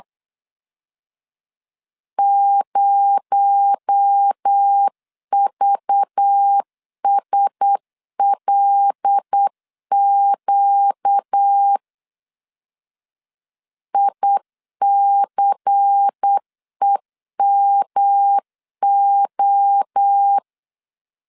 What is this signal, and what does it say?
Morse code, heard as a keyed tone, about 8 wpm, 800 Hz. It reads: 0VSLQ ICEMO
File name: LEZIONI / 6 PKX0 / 6 PKX0 8wpm 3341